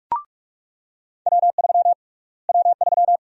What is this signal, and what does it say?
Morse code, heard as a keyed tone, about 46 wpm, 700 Hz. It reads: W3 W3
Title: USA Call Sign Prefixes - Repeated 2x - 45wpm